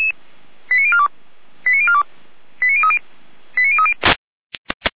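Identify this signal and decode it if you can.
Five-tone selective call (ZVEI), heard as quick runs of five tones, tones of about 70 ms. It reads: 89042 89042 89030 89030
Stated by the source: ZVEI five-tone selective call received on SDR by the Orlamünde fire brigade (feuerwehr-orlamuende).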